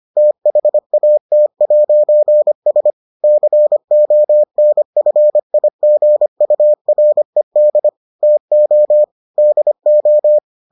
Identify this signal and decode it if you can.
Morse code, heard as a keyed tone, about 25 wpm, 600 Hz. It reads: THAT'S CONFIGURED TO DO